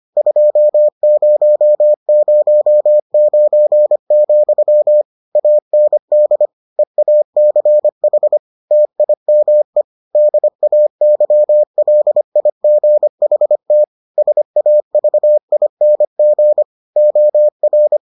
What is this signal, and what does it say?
Morse code, heard as a keyed tone, about 25 wpm, 600 Hz. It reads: 2009, AND EACH TIME DAYLIGHT SAVING OR